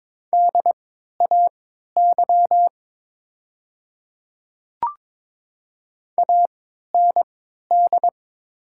Morse code, heard as a keyed tone, about 22 wpm, spaced as for 13 wpm, 700 Hz. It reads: DAY AND
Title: 3x Character Spacing - Top 100 Words - 22wpm